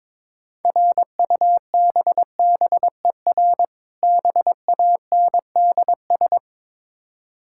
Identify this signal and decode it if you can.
Morse code, heard as a keyed tone, about 22 wpm, 690 Hz. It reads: RUBBER BANDS
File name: Difficult Words with Hints - 22wpm